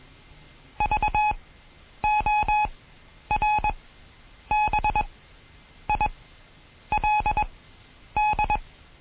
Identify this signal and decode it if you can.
Morse code, heard as a keyed tone, about 21 wpm, spaced as for 9 wpm, 840 Hz. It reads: VORBILD
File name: Morsen / Kurs A 20